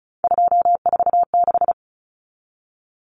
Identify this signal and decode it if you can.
Morse code, heard as a keyed tone, about 34 wpm, 700 Hz. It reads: 246
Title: Common Cut Numbers - 3-Digit 35wpm